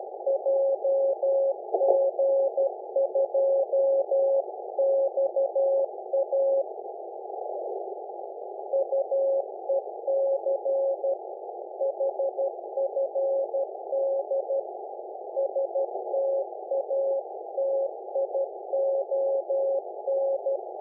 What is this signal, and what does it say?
Morse code, heard as a keyed tone, about 12 wpm, 550 Hz, with noise. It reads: JG2XA UEC HFD STATION
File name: uploader / 5006 jg2xa